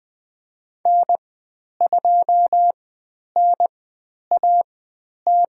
Morse code, heard as a keyed tone, about 20 wpm, spaced as for 10 wpm, 700 Hz. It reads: N2NAT
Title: 102 Lesson 17g - Letter u - Call Signs 20wpm